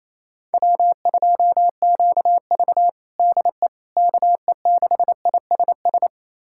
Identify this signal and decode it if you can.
Morse code, heard as a keyed tone, about 28 wpm, 700 Hz. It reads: W2QV DE KE6ISS
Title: USA Callsign DE Callsign 28wpm